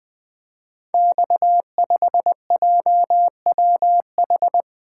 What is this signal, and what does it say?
Morse code, heard as a keyed tone, about 20 wpm, 700 Hz. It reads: X5JWH
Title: Code Groups - 5 Letter:Numbers - 20wpm